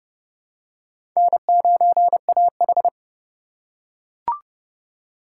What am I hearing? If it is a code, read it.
Morse code, heard as a keyed tone, about 30 wpm, 700 Hz. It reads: N9AH
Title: USA Call Signs - 4 Characters - 3x Character Spacing - 30wpm